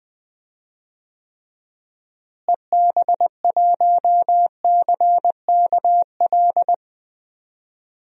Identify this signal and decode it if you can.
Morse code, heard as a keyed tone, about 21 wpm, 700 Hz. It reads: EB1CKL
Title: DXCC-100 Call Signs - 6 Characters - Repeated 2x - 20wpm